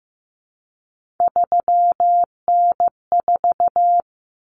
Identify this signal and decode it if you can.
Morse code, heard as a keyed tone, about 15 wpm, 700 Hz. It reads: 3N4
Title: Common Cut Numbers - 3-Digit 15wpm